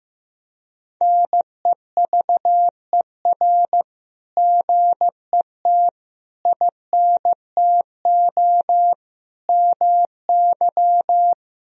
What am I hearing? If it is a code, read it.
Morse code, heard as a keyed tone, about 15 wpm, 700 Hz. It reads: NEVER GET INTO MY
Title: Sentences from Top 1000 Words - Review Part 3 - 15wpm